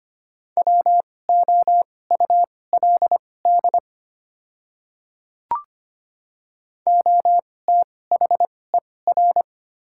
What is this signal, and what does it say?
Morse code, heard as a keyed tone, about 25 wpm, spaced as for 18 wpm, 700 Hz. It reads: WOULD OTHER
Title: 2x Character Spacing - Top 100 Words - 25wpm